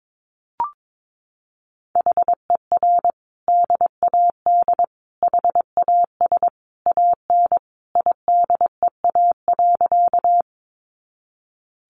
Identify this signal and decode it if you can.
Morse code, heard as a keyed tone, about 22 wpm, 700 Hz. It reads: HER DAD HAS AN IDEA.
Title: Additional Easy Phrases 22wpm